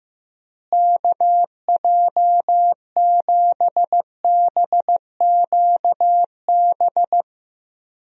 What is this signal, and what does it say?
Morse code, heard as a keyed tone, about 15 wpm, 700 Hz. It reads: KJ7BQB
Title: USA Call Signs - 6 Characters - Repeated 3x - 15wpm